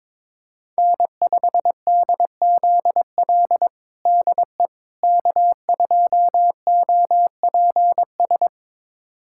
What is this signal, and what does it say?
Morse code, heard as a keyed tone, about 22 wpm, 700 Hz. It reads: N5DZL DE K2OPS
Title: USA Callsign DE Callsign - 5 Characters - 22wpm